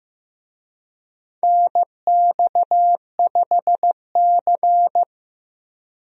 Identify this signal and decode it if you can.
Morse code, heard as a keyed tone, about 15 wpm, 700 Hz. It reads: NX5C
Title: USA Call Signs - 4 Characters - Repeated 2x - 15wpm